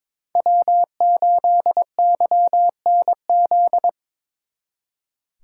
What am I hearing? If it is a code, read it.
Morse code, heard as a keyed tone, about 22 wpm, 700 Hz. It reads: W8YNZ
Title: Speed Racing - USA Call Signs 22wpm